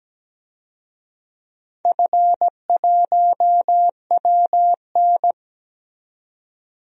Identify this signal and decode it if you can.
Morse code, heard as a keyed tone, about 17 wpm, 700 Hz. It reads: F1WN
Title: DXCC-100 Call Signs - 4 Characters - 17wpm